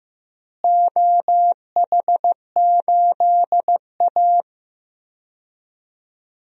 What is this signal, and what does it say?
Morse code, heard as a keyed tone, about 15 wpm, 700 Hz. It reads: OH8A
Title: DXCC-100 Call Signs - 4 Characters - Repeated 2x - 15wpm